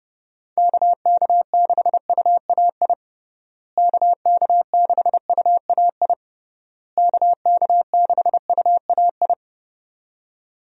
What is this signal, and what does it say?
Morse code, heard as a keyed tone, about 30 wpm, 700 Hz. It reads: KK6UAI KK6UAI KK6UAI
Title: USA Call Signs - 6 Characters - Repeated 3x - 30wpm